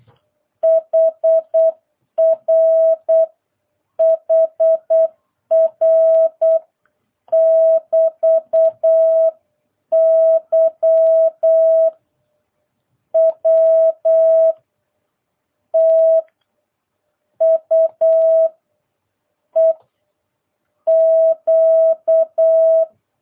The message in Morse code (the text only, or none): HR HR <BT> Y W T U E Q